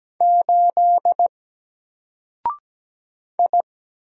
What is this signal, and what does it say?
Morse code, heard as a keyed tone, about 17 wpm, 700 Hz. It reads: 8 I
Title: Single Character (Letter:Number:Punctuation:Prosign) 17wpm